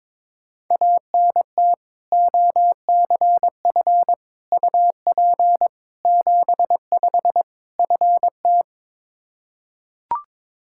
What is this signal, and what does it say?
Morse code, heard as a keyed tone, about 22 wpm, 700 Hz. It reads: ANT OCF UP 75 FT
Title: Practice QSO - Exchange 2 - Info - 22wpm